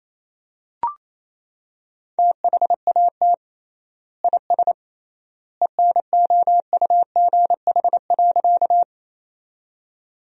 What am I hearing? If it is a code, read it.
Morse code, heard as a keyed tone, about 28 wpm, 700 Hz. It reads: THAT IS ENOUGH.
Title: 3x Word-Spacing - Sentences from Top 400 Words - 28wpm